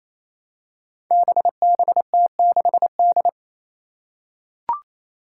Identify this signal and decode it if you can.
Morse code, heard as a keyed tone, about 29 wpm, 700 Hz. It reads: BBT6D
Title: Five Challenging Letter-Number Combinations 28wpm